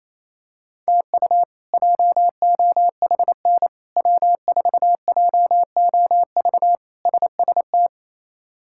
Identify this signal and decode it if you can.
Morse code, heard as a keyed tone, about 28 wpm, 700 Hz. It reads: TU JOHN W4JOV SST
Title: K1USN SST QSO Practice - 28wpm